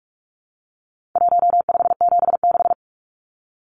Morse code, heard as a keyed tone, about 44 wpm, 700 Hz. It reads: 1576
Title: Numbers - Contest Style - 4-Digit 45wpm